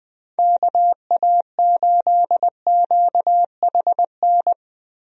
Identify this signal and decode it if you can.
Morse code, heard as a keyed tone, about 21 wpm, 700 Hz. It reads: KA8QHN
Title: USA Call Signs - 6 Characters - Repeated 3x - 20wpm